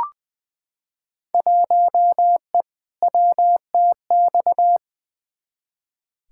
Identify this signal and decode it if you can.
Morse code, heard as a keyed tone, about 20 wpm, 700 Hz. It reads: 1E WTX
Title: Field Day Exchanges - 20wpm